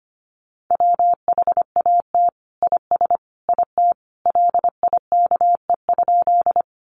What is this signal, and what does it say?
Morse code, heard as a keyed tone, about 25 wpm, 700 Hz. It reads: WHAT IS IT LIKE?
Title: Sentences from Top 100 Words 25wpm